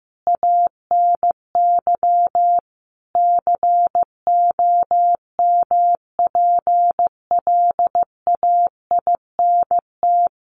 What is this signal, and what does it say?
Morse code, heard as a keyed tone, about 15 wpm, 700 Hz. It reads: ANY COMPLAINT
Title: Sentences from Top 1400 Words 15wpm